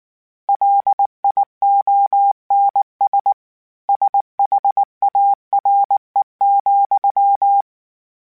Morse code, heard as a keyed tone, about 19 wpm, 800 Hz. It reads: LIONS SHARE,
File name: Word Pairs @18